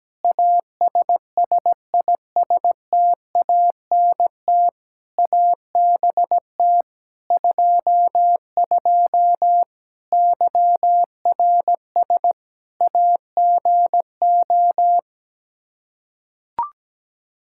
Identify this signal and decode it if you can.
Morse code, heard as a keyed tone, about 17 wpm, 700 Hz. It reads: ASSISTANT ABT 22 YRS AGO
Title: Practice QSO - Exchange 3 - More Info - 17wpm